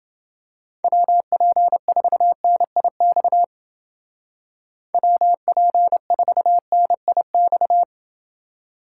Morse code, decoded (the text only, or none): WP4NIX WP4NIX